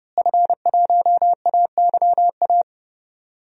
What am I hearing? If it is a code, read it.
Morse code, heard as a keyed tone, about 30 wpm, 700 Hz. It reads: F1AYA
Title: DXCC-100 Call Signs - 5 Characters - Repeated 2x - 30wpm